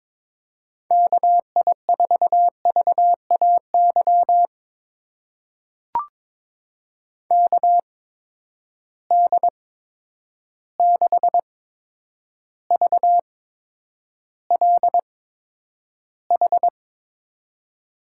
Morse code, heard as a keyed tone, about 22 wpm, spaced as for 9 wpm, 700 Hz. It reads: KI4VAY KD6VLH
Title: USA Call Signs - 6 Characters - 8x Character Spacing - 22wpm